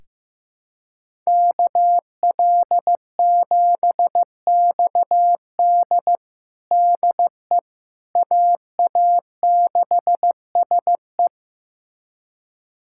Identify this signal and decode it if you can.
Morse code, heard as a keyed tone, about 15 wpm, 700 Hz. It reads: KL7XD DE AA6SE
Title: USA Callsign DE Callsign - 5 Characters - 15wpm